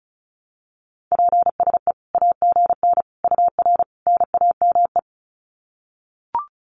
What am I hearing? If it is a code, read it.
Morse code, heard as a keyed tone, about 34 wpm, 700 Hz. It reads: PSE AGN UR NAME
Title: Common QSO Phrases 35wpm